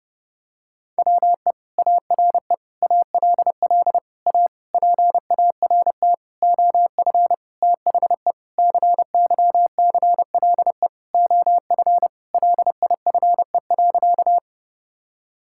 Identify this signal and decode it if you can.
Morse code, heard as a keyed tone, about 30 wpm, 700 Hz. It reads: WE ARE ALL A PART OF THE CYCLE OF LIFE.
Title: Sentences from Top 1000 Words - Review Part 6 - 30wpm